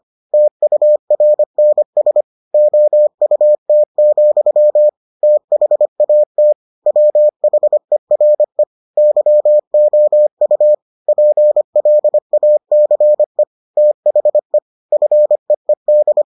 Morse code, heard as a keyed tone, about 25 wpm, 600 Hz. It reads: TURNS OUT, THAT WHERE YOU PLACE THE FEED